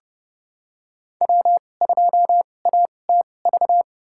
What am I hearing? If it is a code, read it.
Morse code, heard as a keyed tone, about 31 wpm, spaced as for 22 wpm, 700 Hz. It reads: W2ATV
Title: USA Call Signs - 5 Characters - 2x Character Spacing - 30wpm